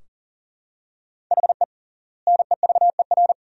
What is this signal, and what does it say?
Morse code, heard as a keyed tone, about 40 wpm, 700 Hz. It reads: HE NEVER
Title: 3x Word-Spacing - Sentences from Top 400 Words - 40wpm